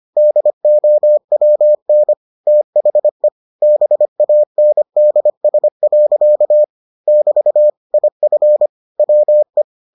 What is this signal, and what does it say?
Morse code, heard as a keyed tone, about 25 wpm, 600 Hz. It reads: DOWN THE BANDS. <BT> IF WE